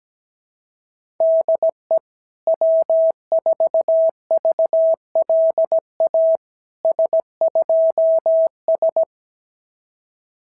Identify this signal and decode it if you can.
Morse code, heard as a keyed tone, about 17 wpm, 650 Hz. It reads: DE W4VLA S2S
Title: SOTA QSO Practice 17wpm